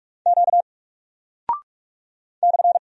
Morse code, heard as a keyed tone, about 44 wpm, 700 Hz. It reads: Q /
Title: Single Character (Letter:Number:Punctuation:Prosign) 45wpm